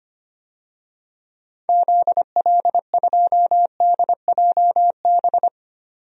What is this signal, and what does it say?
Morse code, heard as a keyed tone, about 25 wpm, 700 Hz. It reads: ZL2DJB
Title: DXCC-100 Call Signs - 6 Characters - 25wpm